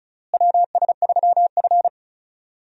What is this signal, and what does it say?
Morse code, heard as a keyed tone, about 34 wpm, 700 Hz. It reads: WS3F
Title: USA Call Signs - 4 Characters - Repeated 2x - 35wpm